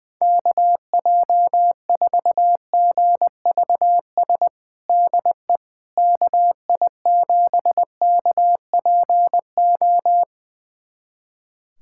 Morse code, heard as a keyed tone, about 20 wpm, 700 Hz. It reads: KJ4GVS DE KI7KPO
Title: USA Callsign DE Callsign - 6 Characters - 20wpm